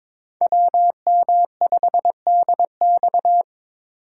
Morse code, heard as a keyed tone, about 22 wpm, 700 Hz. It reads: WM5DX
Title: K1USN SST QSO Practice - 22wpm